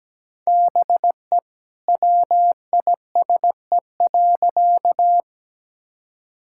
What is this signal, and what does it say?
Morse code, heard as a keyed tone, about 17 wpm, 700 Hz. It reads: BE WISE.
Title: Sentences from Top 1600 Words 17wpm